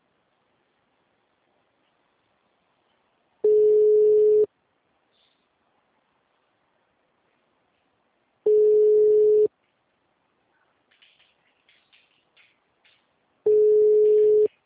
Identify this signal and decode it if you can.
Ringback tone (Europe, 425 Hz).